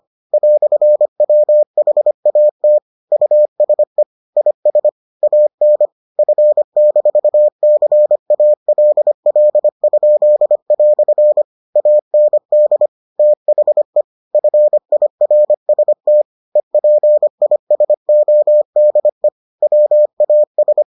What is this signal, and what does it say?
Morse code, heard as a keyed tone, about 25 wpm, 600 Hz. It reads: "WHAT USE IS AN F-CALL?" AND THE FIRST EPISODE WAS